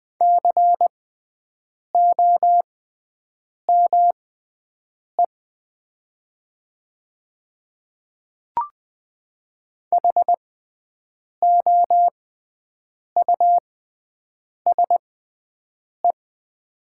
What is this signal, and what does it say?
Morse code, heard as a keyed tone, about 20 wpm, spaced as for 7 wpm, 700 Hz. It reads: COME HOUSE
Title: 6x Character Spacing - Top 200 Words - 20wpm